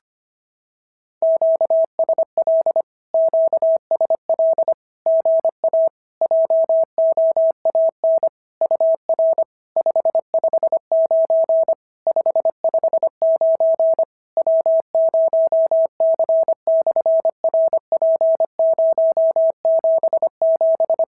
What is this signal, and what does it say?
Morse code, heard as a keyed tone, about 25 wpm, 650 Hz. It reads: QSL QSL GA JOAN UR 559 559 W0C/RP077
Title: SOTA QSO Practice 25wpm